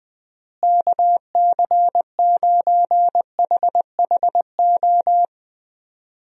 Morse code, heard as a keyed tone, about 20 wpm, 700 Hz. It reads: KC9HHO
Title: USA Call Signs - 6 Characters - Repeated 2x - 20wpm